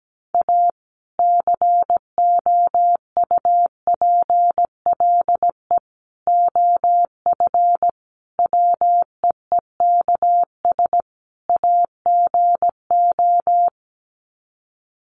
Morse code, heard as a keyed tone, about 17 wpm, 700 Hz. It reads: A COUPLE OF WEEKS AGO
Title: Sets of 5 Words 17wpm